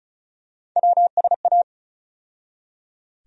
Morse code, heard as a keyed tone, about 34 wpm, 700 Hz. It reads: WSA
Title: USA Call Sign Postfixes - 3 Letters 35wpm